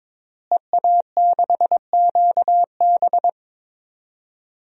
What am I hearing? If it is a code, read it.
Morse code, heard as a keyed tone, about 22 wpm, 700 Hz. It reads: EA6QB
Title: DXCC-100 Call Signs - 5 Characters - Repeated 2x - 22wpm